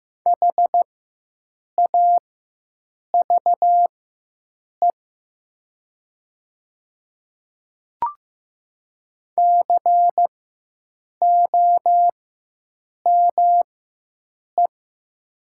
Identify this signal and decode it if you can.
Morse code, heard as a keyed tone, about 15 wpm, spaced as for 7 wpm, 700 Hz. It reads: HAVE COME
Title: 4x Character Spacing - Top 100 Words - 15wpm